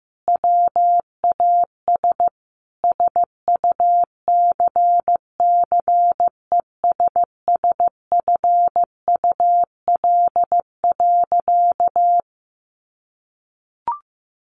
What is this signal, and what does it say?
Morse code, heard as a keyed tone, about 15 wpm, 700 Hz. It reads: WAS SUCCESSFUL.
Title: Sentences from Top 1200 Words 15wpm